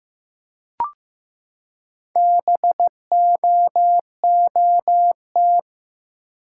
Morse code, heard as a keyed tone, about 15 wpm, 700 Hz. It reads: BOOT
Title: Four Letter Words - 15wpm